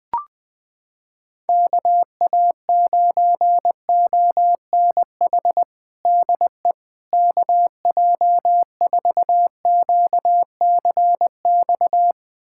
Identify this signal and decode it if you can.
Morse code, heard as a keyed tone, about 20 wpm, 700 Hz. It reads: KA9ONH DE KJ4QCX
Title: USA Callsign DE Callsign - 6 Characters - 20wpm